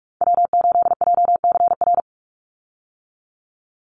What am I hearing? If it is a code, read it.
Morse code, heard as a keyed tone, about 44 wpm, 700 Hz. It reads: W8JCR
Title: USA Call Signs 45wpm